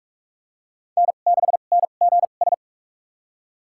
Morse code, heard as a keyed tone, about 43 wpm, spaced as for 32 wpm, 700 Hz. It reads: N6NGS
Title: USA Call Signs - 5 Characters - 2x Character Spacing - 45wpm